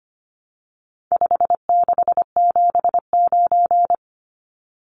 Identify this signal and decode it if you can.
Morse code, heard as a keyed tone, about 25 wpm, 700 Hz. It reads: 5679
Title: Numbers - 4-Digit 25wpm